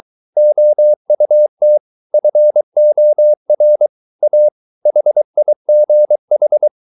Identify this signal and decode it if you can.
Morse code, heard as a keyed tone, about 23 wpm, 600 Hz. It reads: OUT FOR A HIGH